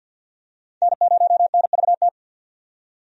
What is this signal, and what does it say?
Morse code, heard as a keyed tone, about 51 wpm, 700 Hz. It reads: N0NVT
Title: USA Call Signs - 5 Characters - 50wpm